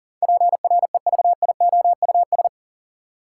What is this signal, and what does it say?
Morse code, heard as a keyed tone, about 40 wpm, 700 Hz. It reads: PREVIOUS
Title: Sentences from Top 1500 Words - New Word Drill Alt - 40wpm